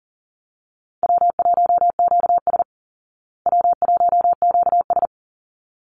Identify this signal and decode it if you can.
Morse code, heard as a keyed tone, about 40 wpm, 700 Hz. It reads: W1QS W1QS
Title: Call Signs Repeated 3x - 40wpm